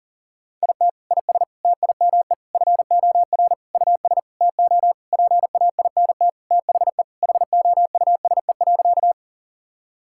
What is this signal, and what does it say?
Morse code, heard as a keyed tone, about 40 wpm, 700 Hz. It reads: IT IS TIME FOR US TO PAINT THE HOUSE.
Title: Sentences from Top 1000 Words - Review Part 6 - 40wpm